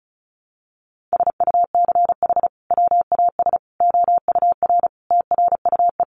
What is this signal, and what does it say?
Morse code, heard as a keyed tone, about 34 wpm, 700 Hz. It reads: SUCH WAS OUR TRUE
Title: Sentences from Top 1200 Words 35wpm